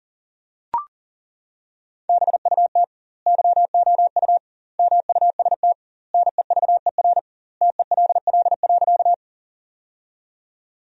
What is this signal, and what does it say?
Morse code, heard as a keyed tone, about 40 wpm, 700 Hz. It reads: BUT YOU MUST NEVER TELL.
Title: 2x Word-Spacing - Sentences from Top 300 Words - 40wpm